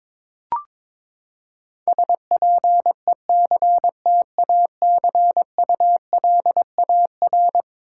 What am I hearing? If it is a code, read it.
Morse code, heard as a keyed tone, about 21 wpm, 700 Hz. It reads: SPECTACULAR
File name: practice / 22wpm / Eleven Letter Words - 22wpm